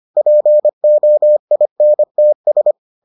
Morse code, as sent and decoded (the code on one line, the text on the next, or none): .--. --- .. -. - ...
POINTS